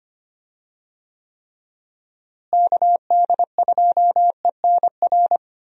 Morse code, heard as a keyed tone, about 25 wpm, 700 Hz. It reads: KD2ENR
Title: USA Call Signs - 6 Characters - Repeated 2x - 25wpm